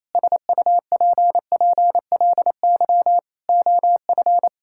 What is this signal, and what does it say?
Morse code, heard as a keyed tone, about 28 wpm, 700 Hz. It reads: SUPPLY OF
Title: Sentences from Top 1000 Words - Review Part 5 - 28wpm